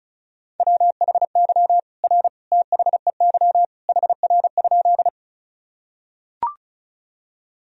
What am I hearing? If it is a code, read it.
Morse code, heard as a keyed tone, about 36 wpm, 700 Hz. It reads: WHY R THEY HR?
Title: Sentences from Top 200 Words and Abbreviations 35wpm